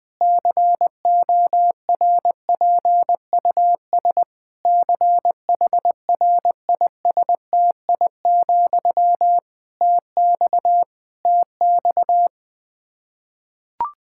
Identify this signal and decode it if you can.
Morse code, heard as a keyed tone, about 20 wpm, 700 Hz. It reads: CORPUS CHRISTI, TX TX
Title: K1USN SST QSO Practice 20wpm